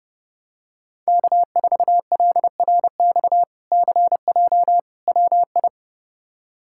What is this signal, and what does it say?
Morse code, heard as a keyed tone, about 30 wpm, 700 Hz. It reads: K4LRX CJ WI
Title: K1USN SST QSO Practice - 30wpm